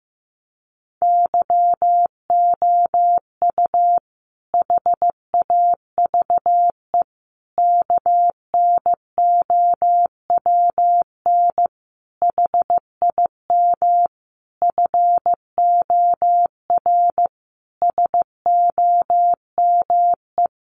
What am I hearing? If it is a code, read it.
Morse code, heard as a keyed tone, about 14 wpm, 700 Hz. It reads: YOU HAVE KNOWN HIM FOR SOME
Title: Sentences from Top 600 Words 15wpm